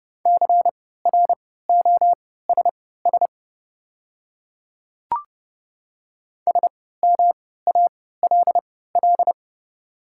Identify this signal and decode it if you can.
Morse code, heard as a keyed tone, about 30 wpm, spaced as for 17 wpm, 690 Hz. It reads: CROSS SMALL
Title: 3x Character Spacing - Top 300 Words - 30wpm